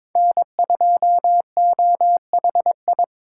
Morse code, heard as a keyed tone, about 26 wpm, 700 Hz. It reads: N2OHI
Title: Speed Racing - USA Call Signs 22wpm